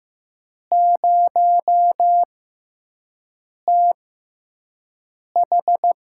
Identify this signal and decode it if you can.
Morse code, heard as a keyed tone, about 15 wpm, spaced as for 5 wpm, 700 Hz. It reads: 0TH
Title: USA Call Signs - 6 Characters - 6x Character Spacing - 15wpm